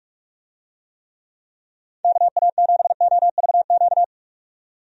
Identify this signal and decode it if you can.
Morse code, heard as a keyed tone, about 41 wpm, 700 Hz. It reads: KA7OVQ
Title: USA Call Signs - 6 Characters - 45wpm